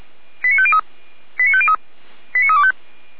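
Five-tone selective call (ZVEI), heard as quick runs of five tones, tones of about 70 ms. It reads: 89503 89503 89325
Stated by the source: ZVEI five-tone selective call received by SDR, logged by Feuerwehr Orlamünde.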